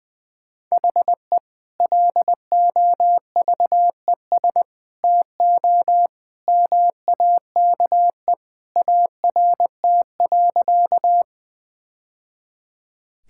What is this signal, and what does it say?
Morse code, heard as a keyed tone, about 20 wpm, 700 Hz. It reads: HE LOVES TO MAKE ART.
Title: Sentences with 2-Word Phrases - 20wpm